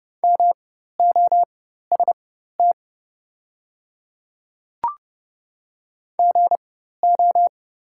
Morse code, heard as a keyed tone, about 30 wpm, spaced as for 14 wpm, 690 Hz. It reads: MOST GO
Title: 4x Character Spacing - Top 100 Words - 30wpm